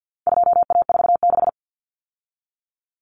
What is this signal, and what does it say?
Morse code, heard as a keyed tone, about 48 wpm, 700 Hz. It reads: 2A46
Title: Common Cut Numbers - 4-Digit 50wpm